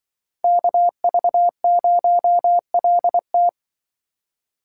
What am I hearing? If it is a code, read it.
Morse code, heard as a keyed tone, about 24 wpm, 700 Hz. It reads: KV0LT
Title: Speed Racing - USA Call Signs 20wpm